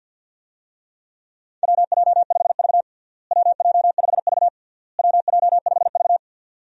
Morse code, heard as a keyed tone, about 51 wpm, 700 Hz. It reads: WJ5V WJ5V WJ5V
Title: USA Call Signs - 4 Characters - Repeated 3x - 50wpm